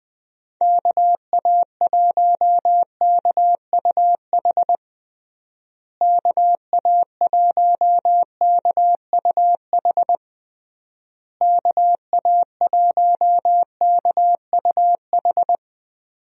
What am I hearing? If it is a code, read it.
Morse code, heard as a keyed tone, about 21 wpm, 700 Hz. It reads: KA1KUH KA1KUH KA1KUH
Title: USA Call Signs - 6 Characters - Repeated 3x - 20wpm